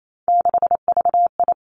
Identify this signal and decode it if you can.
Morse code, heard as a keyed tone, about 28 wpm, 700 Hz. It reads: 6VI
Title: Three Challenging Letter-Number Combinations 28wpm